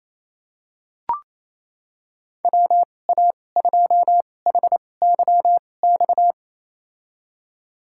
Morse code, heard as a keyed tone, about 27 wpm, spaced as for 20 wpm, 700 Hz. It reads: WA2HYX
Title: USA Call Signs - 6 Characters - 2x Character Spacing - 28wpm